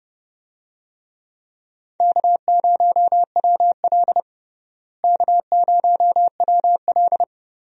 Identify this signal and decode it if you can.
Morse code, heard as a keyed tone, about 29 wpm, 700 Hz. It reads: K0WL K0WL
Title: USA Call Signs - 4 Characters - Repeated 3x - 30wpm